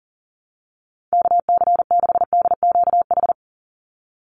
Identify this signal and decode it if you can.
Morse code, heard as a keyed tone, about 40 wpm, 700 Hz. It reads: KC6DQH
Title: USA Call Signs 40wpm